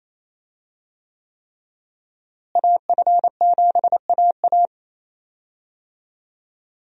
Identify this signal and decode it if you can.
Morse code, heard as a keyed tone, about 28 wpm, 700 Hz. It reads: AF7AA
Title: USA Call Signs - 5 Characters - Repeated 3x - 28wpm